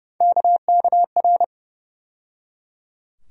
Morse code, heard as a keyed tone, about 30 wpm, 700 Hz. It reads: KKR
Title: USA Call Sign Postfixes - 3 Letters 30wpm